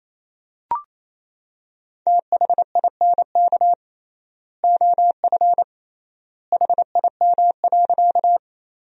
Morse code, heard as a keyed tone, about 28 wpm, 700 Hz. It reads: THINK OF HIM.
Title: 3x Word-Spacing - Sentences from Top 200 Words - 28wpm